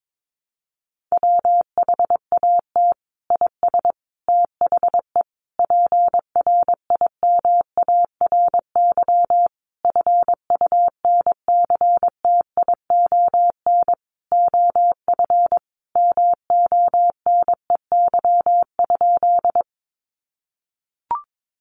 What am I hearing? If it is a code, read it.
Morse code, heard as a keyed tone, about 22 wpm, 700 Hz. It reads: WHAT IS THE PRIMARY FUNCTION OF MONEY?
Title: Sentences from Top 1100 Words 22wpm